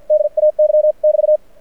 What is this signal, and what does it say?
Morse code, heard as a keyed tone, about 51 wpm, spaced as for 44 wpm, 600 Hz. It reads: DAY <BT>
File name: archived / CRSnet